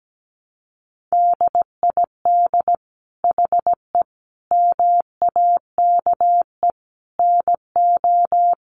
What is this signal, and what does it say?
Morse code, heard as a keyed tone, about 17 wpm, 700 Hz. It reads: DID HE MAKE NO
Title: Sentences from Top 1000 Words 17wpm